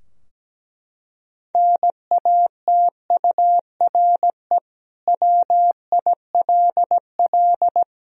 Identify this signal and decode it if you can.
Morse code, heard as a keyed tone, about 17 wpm, 700 Hz. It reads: NATURE WILL
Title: Sentences from Top 1000 Words - Review Part 3 - 17wpm